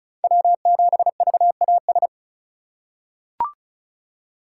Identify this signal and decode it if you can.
Morse code, heard as a keyed tone, about 36 wpm, 700 Hz. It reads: W7VAS
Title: K1USN SST QSO Practice - 35wpm